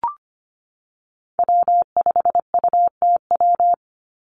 Morse code, heard as a keyed tone, about 25 wpm, 700 Hz. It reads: W5UTW
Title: Call Signs Repeated 3x - 25wpm